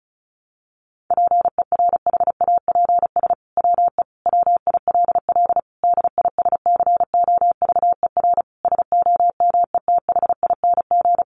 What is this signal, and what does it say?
Morse code, heard as a keyed tone, about 34 wpm, 700 Hz. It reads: PERHAPS WE WILL DISCOVER SOMETHING
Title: Sentences from Top 1100 Words 35wpm